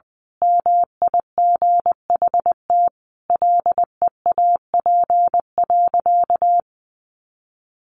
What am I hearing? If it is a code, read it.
Morse code, heard as a keyed tone, about 20 wpm, 700 Hz. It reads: MIGHT LEAP.
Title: A Sound Of Thunder 20wpm